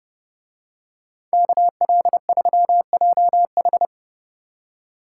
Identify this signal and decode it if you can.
Morse code, heard as a keyed tone, about 28 wpm, 700 Hz. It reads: KL3JH